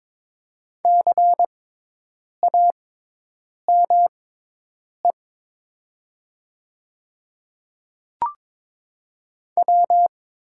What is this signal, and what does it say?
Morse code, heard as a keyed tone, about 22 wpm, spaced as for 8 wpm, 700 Hz. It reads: CAME W
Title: 6x Character Spacing - Top 300 Words - 22wpm